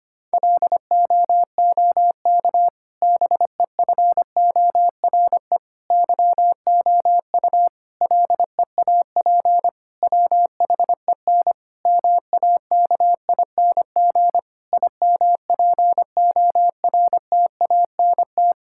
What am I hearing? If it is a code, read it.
Morse code, heard as a keyed tone, about 25 wpm, 700 Hz. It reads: LOOK BEFORE YOU LEAP WHEN MAKING IMPORTANT
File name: practice / 25wpm / Difficult Sentences with Hints - 25wpm